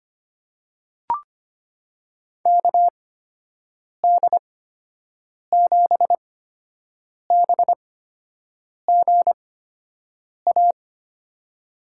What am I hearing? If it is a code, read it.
Morse code, heard as a keyed tone, about 25 wpm, spaced as for 7 wpm, 700 Hz. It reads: KD7BGA
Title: USA Call Signs - 6 Characters - 8x Character Spacing - 25wpm